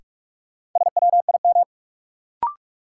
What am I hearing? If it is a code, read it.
Morse code, heard as a keyed tone, about 45 wpm, 700 Hz. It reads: SWIM
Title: Sentences from Top 1000 Words - Review Part 6 - New Word Drill - 45wpm